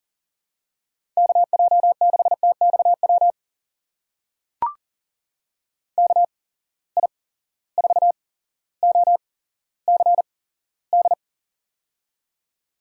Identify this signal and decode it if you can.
Morse code, heard as a keyed tone, about 39 wpm, spaced as for 17 wpm, 700 Hz. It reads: KJ6TXW KI4OCD
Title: USA Call Signs - 6 Characters - 8x Character Spacing - 40wpm